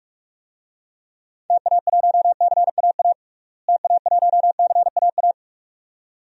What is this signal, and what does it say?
Morse code, heard as a keyed tone, about 41 wpm, 700 Hz. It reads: TA1CAA TA1CAA
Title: DXCC-100 Call Signs - 6 Characters - Repeated 2x - 45wpm